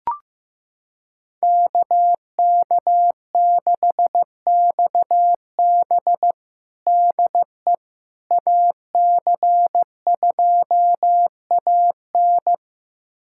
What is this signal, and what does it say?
Morse code, heard as a keyed tone, about 15 wpm, 700 Hz. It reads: KK6XB DE AC2AN
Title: USA Callsign DE Callsign - 5 Characters - 15wpm